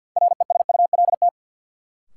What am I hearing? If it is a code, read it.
Morse code, heard as a keyed tone, about 52 wpm, 700 Hz. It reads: RESULT
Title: Sentences from Top 700 Words - New Word Drill - 50wpm